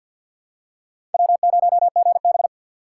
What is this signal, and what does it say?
Morse code, heard as a keyed tone, about 51 wpm, 700 Hz. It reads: W0GB
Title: USA Call Signs - 4 Characters - 2x Character Spacing - 50wpm